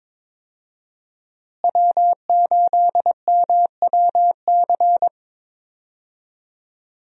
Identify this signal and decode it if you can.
Morse code, heard as a keyed tone, about 22 wpm, 700 Hz. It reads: W8MWC
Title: USA Call Signs - 5 Characters - Repeated 2x - 22wpm